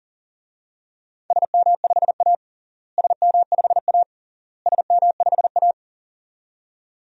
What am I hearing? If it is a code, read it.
Morse code, heard as a keyed tone, about 40 wpm, 700 Hz. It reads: SM5A SM5A SM5A
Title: DXCC-100 Call Signs - 4 Characters - Repeated 3x - 40wpm